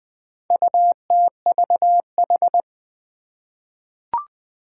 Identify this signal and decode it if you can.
Morse code, heard as a keyed tone, about 20 wpm, 700 Hz. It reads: UTVH
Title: Four Challenging Letter-Number Combinations 20wpm